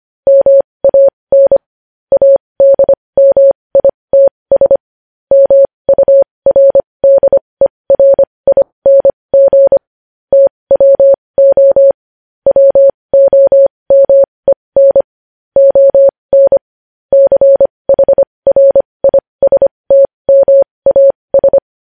Morse code, heard as a keyed tone, about 25 wpm, spaced as for 20 wpm, 550 Hz. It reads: MAN ADMITS MURDERING TWO WOMEN ON CHRISTMAS
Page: news-headlines-in-morse-code-at-20-wpm